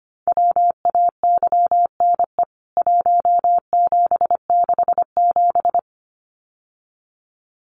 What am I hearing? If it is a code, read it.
Morse code, heard as a keyed tone, about 25 wpm, 700 Hz. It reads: WAYNE 1767
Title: CWT Exchange 25wpm